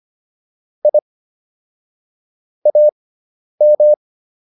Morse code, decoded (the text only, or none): I AM